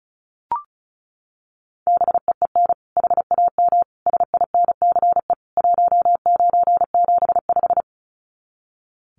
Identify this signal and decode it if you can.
Morse code, heard as a keyed tone, about 34 wpm, 700 Hz. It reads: BEEN HAM SINCE 1975
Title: Common QSO Phrases 35wpm